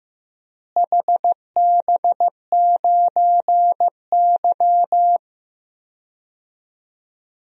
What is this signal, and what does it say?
Morse code, heard as a keyed tone, about 15 wpm, 700 Hz. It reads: HB9Y
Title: DXCC-100 Call Signs - 4 Characters - Repeated 3x - 15wpm